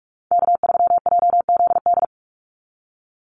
Morse code, heard as a keyed tone, about 46 wpm, 690 Hz. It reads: K3JZD
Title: NA SOTA Chaser Callsigns 45wpm